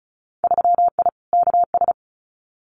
Morse code, heard as a keyed tone, about 34 wpm, 700 Hz. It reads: 3I KS
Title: Winter Field Day Exchanges 35wpm